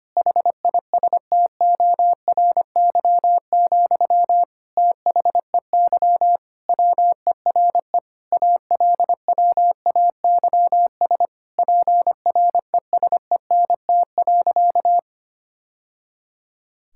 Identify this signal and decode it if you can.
Morse code, heard as a keyed tone, about 25 wpm, 700 Hz. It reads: HISTORY, THEY WERE ALWAYS PRESENT.
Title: Sentences from Top 1000 Words - Review Part 5 - 25wpm